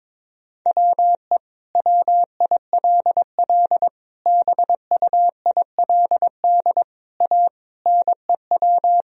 Morse code, heard as a keyed tone, about 22 wpm, 700 Hz. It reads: WE WILL BUILD A NEW
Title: Sentences from Top 1000 Words - Review Part 3 - 22wpm